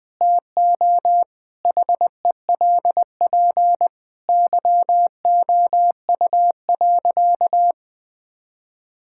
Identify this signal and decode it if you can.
Morse code, heard as a keyed tone, about 20 wpm, 700 Hz. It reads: TO HELP YOU.
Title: Sentences with 4-Word Phrases - 20wpm